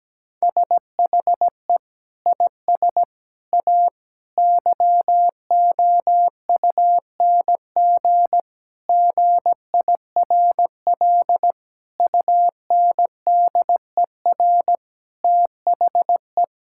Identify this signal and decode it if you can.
Morse code, heard as a keyed tone, about 17 wpm, 700 Hz. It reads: SHE IS A YOUNG GIRL UNDER THE
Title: Sentences from Top 1600 Words 17wpm